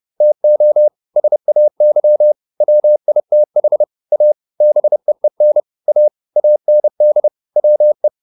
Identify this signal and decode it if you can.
Morse code, heard as a keyed tone, about 30 wpm, 600 Hz. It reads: TO SAY WITH A BEEN A AND WE